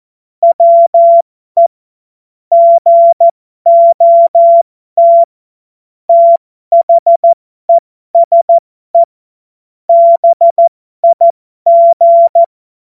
Morse code, heard as a keyed tone, about 14 wpm, spaced as for 12 wpm, 680 Hz. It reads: WE GOT THESE BIG
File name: Morsekurs / Witze